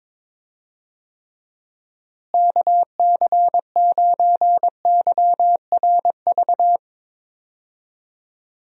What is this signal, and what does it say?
Morse code, heard as a keyed tone, about 22 wpm, 700 Hz. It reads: KC9YRV
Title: USA Call Signs - 6 Characters - Repeated 2x - 22wpm